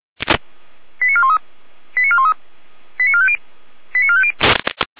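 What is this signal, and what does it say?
Five-tone selective call (ZVEI), heard as quick runs of five tones, tones of about 70 ms. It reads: 89413 89413 89450 89450
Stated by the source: Five-tone ZVEI selective call received on an SDR.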